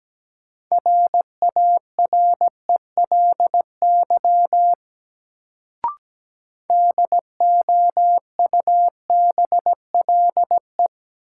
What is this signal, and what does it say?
Morse code, heard as a keyed tone, about 17 wpm, 700 Hz. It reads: RARELY DOUBLE
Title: Sentences from Top 1400 Words - New Word Drill Alt - 17wpm